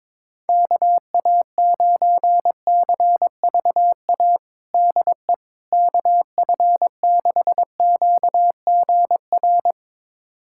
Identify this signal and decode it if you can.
Morse code, heard as a keyed tone, about 22 wpm, 700 Hz. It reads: KA9CVA DE KF6QGR
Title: USA Callsign DE Callsign 22wpm